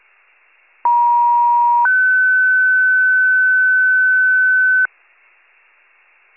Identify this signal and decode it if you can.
Two-tone page, heard as a one-second tone, then a three-second tone, about 950 then 1500 Hz. It is heard from 0.8 to 4.9 s.